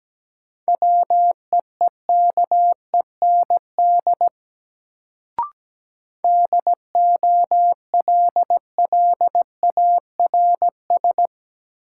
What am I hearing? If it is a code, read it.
Morse code, heard as a keyed tone, about 17 wpm, 700 Hz. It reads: WEEKEND DOLLARS
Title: Sets of 5 Words - New Word Drill - 17wpm